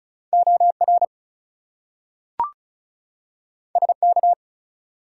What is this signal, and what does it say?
Morse code, heard as a keyed tone, about 36 wpm, 700 Hz. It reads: OR SK
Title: US States and Canadian Provinces 35wpm